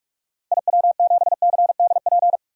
Morse code, heard as a keyed tone, about 44 wpm, 700 Hz. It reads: IW7CDP
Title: DXCC-100 Call Signs - 6 Characters - 45wpm